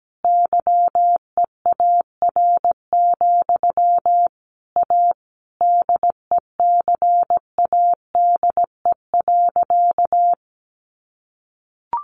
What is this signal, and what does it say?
Morse code, heard as a keyed tone, about 17 wpm, 700 Hz. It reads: YEAR, A DECADE.
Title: A Sound Of Thunder 17wpm